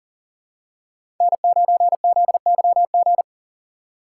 Morse code, heard as a keyed tone, about 39 wpm, 700 Hz. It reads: N9ZYG
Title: USA Call Signs - 5 Characters - 2x Character Spacing - 40wpm